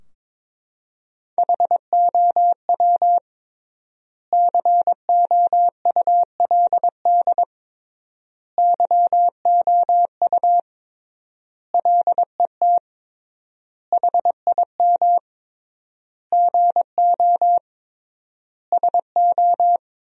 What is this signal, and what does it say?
Morse code, heard as a keyed tone, about 22 wpm, 700 Hz. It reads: HOW COULD YOU LET HIM GO SO
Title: 3x Word-Spacing - Sentences from Top 300 Words - 22wpm